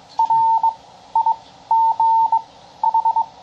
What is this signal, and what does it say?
Morse code, heard as a keyed tone, about 22 wpm, spaced as for 13 wpm, 900 Hz. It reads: RIGH
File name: CW Learning Programs / sounds and personalities of cw and transmitters / 15 - Bug